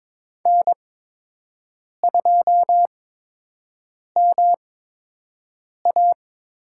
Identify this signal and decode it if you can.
Morse code, heard as a keyed tone, about 22 wpm, spaced as for 6 wpm, 700 Hz. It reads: N2MA
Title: USA Call Signs - 5 Characters - 8x Character Spacing - 22wpm